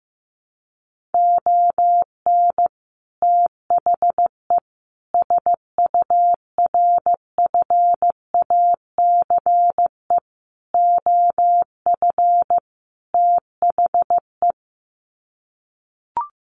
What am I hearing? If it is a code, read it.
Morse code, heard as a keyed tone, about 15 wpm, 700 Hz. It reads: ON THE SURFACE OF THE
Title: Sets of 5 Words Encore 15wpm